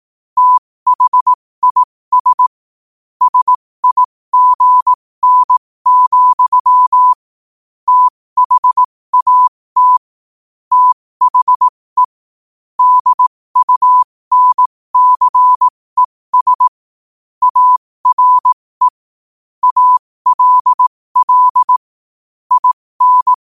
Morse code, heard as a keyed tone, about 18 wpm, spaced as for 15 wpm, 1000 Hz. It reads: THIS SIGN, THAT THE DUNCES ARE ALL IN
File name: qotdfiles / 15wpm